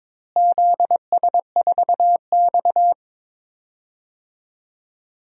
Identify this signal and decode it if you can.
Morse code, heard as a keyed tone, about 22 wpm, 700 Hz. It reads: ZS4X
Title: DXCC-100 Call Signs - 4 Characters - Repeated 2x - 22wpm